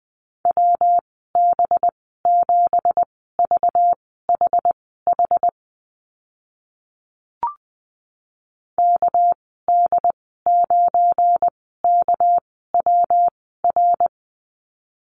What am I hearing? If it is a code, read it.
Morse code, heard as a keyed tone, about 20 wpm, spaced as for 14 wpm, 700 Hz. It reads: WB7VHH KD9KWR
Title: 207 Lesson 33g - Letter j - Call Signs 20wpm